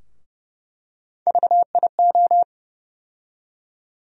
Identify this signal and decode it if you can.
Morse code, heard as a keyed tone, about 29 wpm, 700 Hz. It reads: VIO
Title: USA Call Sign Postfixes - 3 Letters 30wpm